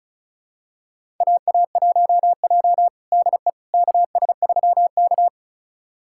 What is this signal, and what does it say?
Morse code, heard as a keyed tone, about 36 wpm, 700 Hz. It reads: AA1J DE KS3K
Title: USA Callsign DE Callsign - 4 Characters - 35wpm